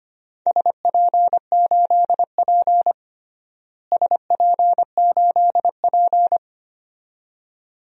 Morse code, heard as a keyed tone, about 25 wpm, 700 Hz. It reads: SP8P SP8P
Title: DXCC-100 Call Signs - 4 Characters - Repeated 2x - 25wpm